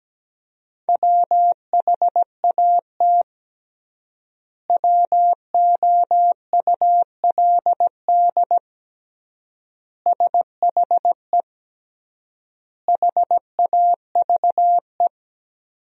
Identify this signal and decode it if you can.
Morse code, heard as a keyed tone, about 17 wpm, 700 Hz. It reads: WHAT WOULD SHE HAVE
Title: 3x Word-Spacing - Sentences from Top 100 Words - 17wpm